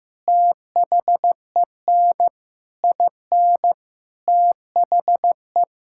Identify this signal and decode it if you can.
Morse code, heard as a keyed tone, about 15 wpm, 700 Hz. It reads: THEN IN THE
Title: Sentences from Top 1600 Words 15wpm